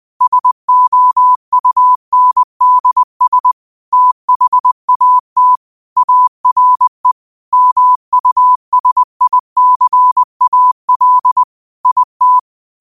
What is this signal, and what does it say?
Morse code, heard as a keyed tone, about 20 wpm, 1000 Hz. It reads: SOUNDS THAT ARE MUSICAL IT
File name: qotdfiles / 20wpm